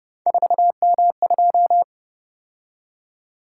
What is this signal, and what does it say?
Morse code, heard as a keyed tone, about 31 wpm, 700 Hz. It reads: 4M2